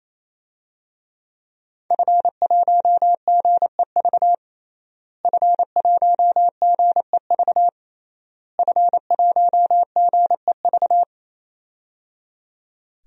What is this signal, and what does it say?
Morse code, heard as a keyed tone, about 28 wpm, 700 Hz. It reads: F1GEV F1GEV F1GEV
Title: DXCC-100 Call Signs - 5 Characters - Repeated 3x - 28wpm